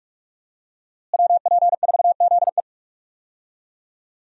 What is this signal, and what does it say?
Morse code, heard as a keyed tone, about 42 wpm, 700 Hz. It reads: WP4ZE
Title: USA Call Signs - 5 Characters - Repeated 3x - 45wpm